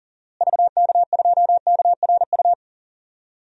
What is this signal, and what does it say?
Morse code, heard as a keyed tone, about 38 wpm, 700 Hz. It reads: VK2KRU
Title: DXCC-100 Call Signs - 6 Characters - Repeated 2x - 40wpm